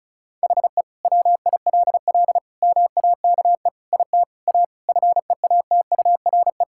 Morse code, heard as a keyed tone, about 36 wpm, 700 Hz. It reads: HE WILL MAKE IT A FEATURE
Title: Sentences from Top 1000 Words - Review Part 5 - 35wpm